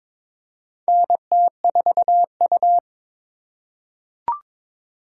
Morse code, heard as a keyed tone, about 22 wpm, 700 Hz. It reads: NT4U
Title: USA Call Signs - 4 Characters - 3x Character Spacing - 22wpm